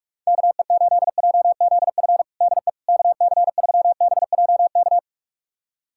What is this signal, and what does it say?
Morse code, heard as a keyed tone, about 42 wpm, 690 Hz. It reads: KE8JZF DE KC3BJK
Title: USA Callsign DE Callsign - 6 Characters - 45wpm